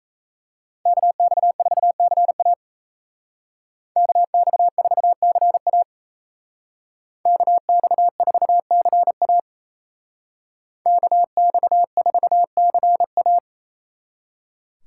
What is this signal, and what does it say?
Morse code, heard as a keyed tone, about 37 wpm, 700 Hz. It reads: KX4CA KX4CA KX4CA KX4CA